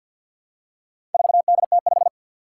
Morse code, heard as a keyed tone, about 52 wpm, 700 Hz. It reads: 4DT5
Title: Four Challenging Letter-Number Combinations 50wpm